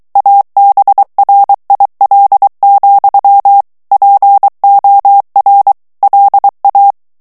Morse code, heard as a keyed tone, about 23 wpm, 800 Hz. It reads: ABRIL, POR LA
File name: ESPAGNOL TEXTE EN CLAIR 2 VITESSE 22 MOTS MINUTES